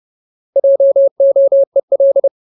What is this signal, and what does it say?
Morse code, heard as a keyed tone, about 30 wpm, 550 Hz. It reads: JOEL
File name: Ultimate Word List / 30WPM Ultimate